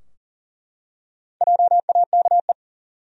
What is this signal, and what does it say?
Morse code, heard as a keyed tone, about 40 wpm, 700 Hz. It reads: JAKE
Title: USA Common Names - Max 5 Letters - 40wpm